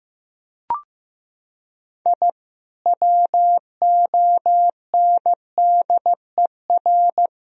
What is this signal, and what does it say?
Morse code, heard as a keyed tone, about 15 wpm, 700 Hz. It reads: I WONDER
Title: Sentences from Top 1000 Words - Review Part 3 - 15wpm